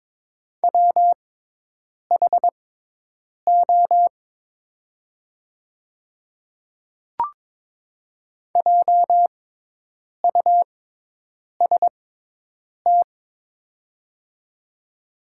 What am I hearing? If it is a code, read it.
Morse code, heard as a keyed tone, about 23 wpm, spaced as for 8 wpm, 700 Hz. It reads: WHO JUST
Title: 6x Character Spacing - Top 300 Words - 22wpm